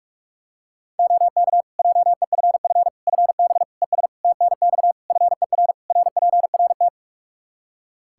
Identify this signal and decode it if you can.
Morse code, heard as a keyed tone, about 41 wpm, 700 Hz. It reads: OK JEFF FB ES TNX FER RPRT
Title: K1USN SST QSO Practice 45wpm